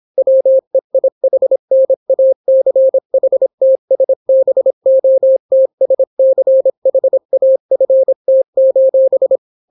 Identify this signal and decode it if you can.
Morse code, heard as a keyed tone, about 26 wpm, 530 Hz. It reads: WEIHNACHTSBOTSCHAFT: